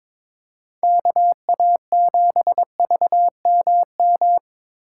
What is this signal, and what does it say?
Morse code, heard as a keyed tone, about 22 wpm, 700 Hz. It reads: KA7VMM
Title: USA Call Signs - 6 Characters - 2x Character Spacing - 22wpm